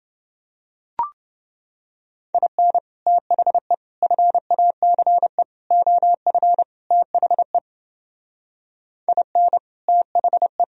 Morse code, heard as a keyed tone, about 29 wpm, 700 Hz. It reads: IN THE FACE OF THE IN THE